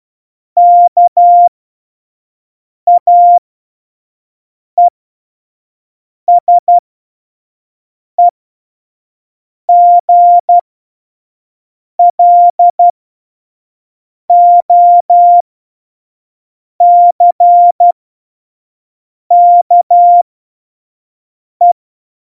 Morse code, heard as a keyed tone, about 12 wpm, spaced as for 5 wpm, 700 Hz. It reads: KAESEGLOCKE
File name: Morsekurs / Witze / Witz 03 April 2018, 25 F70